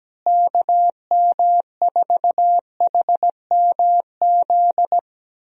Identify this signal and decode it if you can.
Morse code, heard as a keyed tone, about 17 wpm, 700 Hz. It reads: KM4HMZ
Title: USA Call Signs - 6 Characters - Repeated 3x - 17wpm